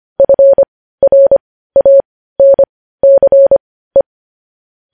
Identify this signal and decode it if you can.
Morse code, heard as a keyed tone, about 25 wpm, spaced as for 15 wpm, 550 Hz. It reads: FRANCE
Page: news headlines in morse code at 15 wpm